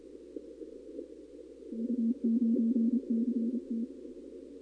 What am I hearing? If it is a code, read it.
Morse code, heard as a keyed tone, about 29 wpm, 230 Hz. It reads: U9CT
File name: contests / 2019-cw-wpx-kt5j-k5pi- -2330 0-32